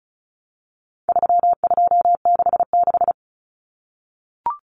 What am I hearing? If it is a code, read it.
Morse code, heard as a keyed tone, about 34 wpm, 700 Hz. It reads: 3266
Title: Numbers - 4-Digit 35wpm